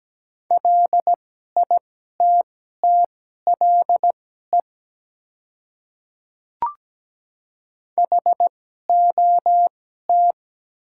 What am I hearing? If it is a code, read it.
Morse code, heard as a keyed tone, about 17 wpm, spaced as for 12 wpm, 700 Hz. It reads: LITTLE HOT
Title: 2x Character Spacing - Top 300 Words - 17wpm